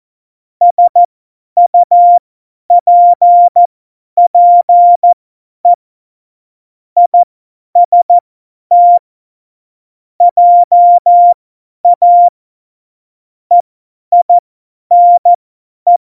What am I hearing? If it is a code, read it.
Morse code, heard as a keyed tone, about 14 wpm, spaced as for 10 wpm, 700 Hz. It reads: SUPPE IST JA EINE
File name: Morsekurs / Witze / Witz 24. September 2019, 50 F70